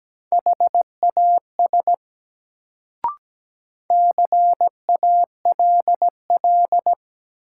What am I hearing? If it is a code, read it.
Morse code, heard as a keyed tone, about 17 wpm, 700 Hz. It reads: HAS CALL
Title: Sentences from Top 100 Words - New Word Drill - 17wpm